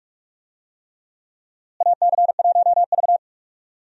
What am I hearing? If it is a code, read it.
Morse code, heard as a keyed tone, about 41 wpm, 700 Hz. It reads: AC1V
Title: USA Call Signs - 4 Characters - Repeated 3x - 45wpm